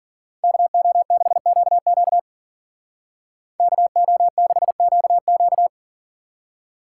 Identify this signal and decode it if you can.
Morse code, heard as a keyed tone, about 48 wpm, 700 Hz. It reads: KO6QQ KO6QQ
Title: Speed Racing - USA 2x2 Call Signs 40wpm